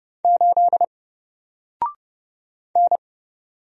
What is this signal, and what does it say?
Morse code, heard as a keyed tone, about 30 wpm, 700 Hz. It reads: 8 N
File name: Single-Letter-Number-30wpm